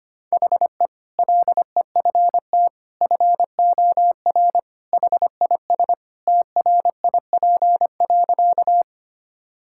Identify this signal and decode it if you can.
Morse code, heard as a keyed tone, about 25 wpm, 700 Hz. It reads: HE LEFT FOR HIS TRIP.
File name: practice / 25wpm / Sentences with 2-Word Phrases - 25wpm